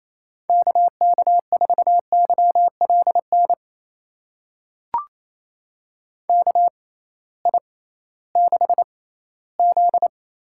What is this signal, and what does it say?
Morse code, heard as a keyed tone, about 28 wpm, spaced as for 16 wpm, 700 Hz. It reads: KK4YLN KI6Z
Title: USA Call Signs - 6 Characters - 6x Character Spacing - 28wpm